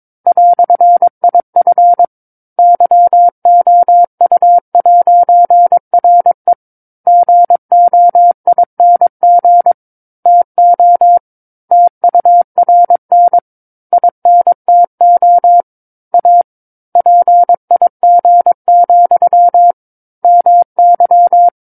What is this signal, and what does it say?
Morse code, heard as a keyed tone, about 22 wpm, 700 Hz. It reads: "IF YOU'RE GOING TO TURN INTO A PIG, MY
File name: 06-Chapter-06 22 WPM